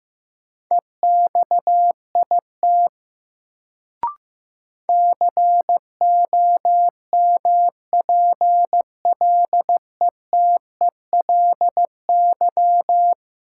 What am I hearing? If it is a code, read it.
Morse code, heard as a keyed tone, about 15 wpm, 700 Hz. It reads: EXIT COMPLETELY
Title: Sentences from Top 1100 Words - New Word Drill - 15wpm